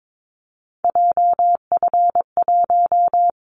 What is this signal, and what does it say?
Morse code, heard as a keyed tone, about 22 wpm, 700 Hz. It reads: JF1
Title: Callsign Prefixes for Top 25 DXCC - 22wpm